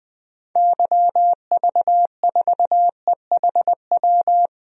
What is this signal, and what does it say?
Morse code, heard as a keyed tone, about 20 wpm, 700 Hz. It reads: YV4EHW
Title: DXCC-100 Call Signs - 6 Characters - 20wpm